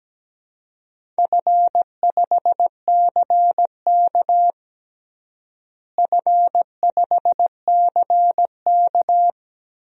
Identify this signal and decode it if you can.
Morse code, heard as a keyed tone, about 17 wpm, 700 Hz. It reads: F5CK F5CK
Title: DXCC-100 Call Signs - 4 Characters - Repeated 2x - 17wpm